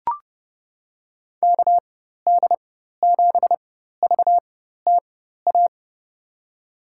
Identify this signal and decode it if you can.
Morse code, heard as a keyed tone, about 29 wpm, spaced as for 14 wpm, 700 Hz. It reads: KD7VTA
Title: USA Call Signs - 6 Characters - 4x Character Spacing - 30wpm